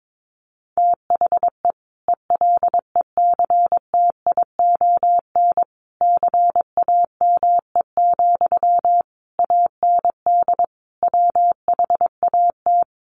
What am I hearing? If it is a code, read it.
Morse code, heard as a keyed tone, about 22 wpm, 700 Hz. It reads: THE ELECTION CAME, AND WHAT
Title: Sentences from Top 1500 Words 22wpm